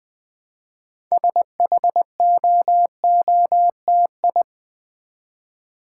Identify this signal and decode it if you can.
Morse code, heard as a keyed tone, about 20 wpm, 700 Hz. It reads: SHOOTI
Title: Build A Word - 2 Letters At A Time - 8-Letter Words - 20wpm